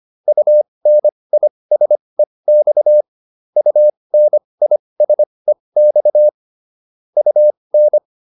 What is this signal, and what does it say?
Morse code, heard as a keyed tone, about 25 wpm, spaced as for 20 wpm, 600 Hz. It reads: UNISEX UNISEX UN
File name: UN6 20